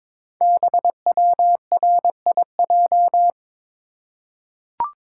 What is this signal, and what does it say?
Morse code, heard as a keyed tone, about 22 wpm, 700 Hz. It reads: BWRIJ
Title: Code Groups - 5 Letters - 22wpm